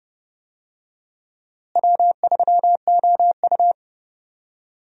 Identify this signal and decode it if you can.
Morse code, heard as a keyed tone, about 32 wpm, 700 Hz. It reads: W3OU
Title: USA Call Signs - 4 Characters - Repeated 2x - 30wpm